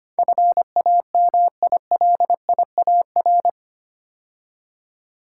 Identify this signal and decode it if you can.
Morse code, heard as a keyed tone, about 25 wpm, 690 Hz. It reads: FAMILIAR
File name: 8 Letter Words - First 3 Letters Given - 25wpm